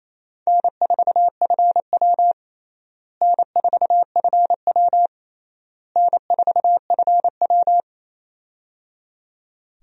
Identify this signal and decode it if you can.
Morse code, heard as a keyed tone, about 27 wpm, 700 Hz. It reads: N4FW N4FW N4FW
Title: USA Call Signs - 4 Characters - Repeated 3x - 28wpm